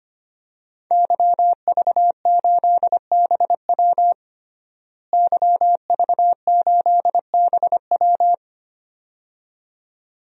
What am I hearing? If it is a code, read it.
Morse code, heard as a keyed tone, about 25 wpm, 700 Hz. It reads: YV8BW YV8BW
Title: DXCC-100 Call Signs - 5 Characters - Repeated 2x - 25wpm